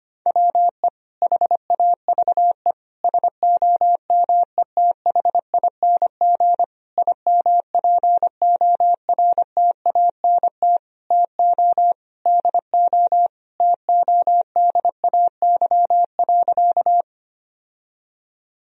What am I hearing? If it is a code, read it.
Morse code, heard as a keyed tone, about 25 wpm, 700 Hz. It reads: WE HAVE SOMETHING IMPORTANT TO DO TODAY.
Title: Sentences from Top 1000 Words - Review Part 4 - 25wpm